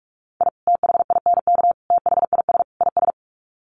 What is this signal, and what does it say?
Morse code, heard as a keyed tone, about 44 wpm, 700 Hz. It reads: I THINK THIS IS
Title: Sets of 4 Words 45wpm